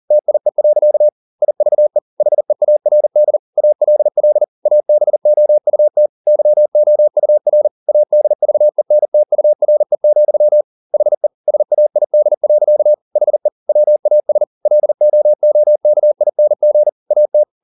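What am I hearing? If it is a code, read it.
Morse code, heard as a keyed tone, about 40 wpm, 600 Hz. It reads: TIE. IVE HEARD ALL ABOUT YOUR ADVENTURE, HE SAID. HE WAS LOOKING AT
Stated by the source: Morse code at 40 wpm, practice text, CW med Gnister.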